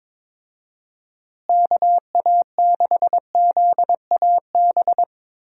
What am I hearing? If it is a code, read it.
Morse code, heard as a keyed tone, about 22 wpm, 700 Hz. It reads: KA6ZAB
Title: USA Call Signs - 6 Characters - Repeated 2x - 22wpm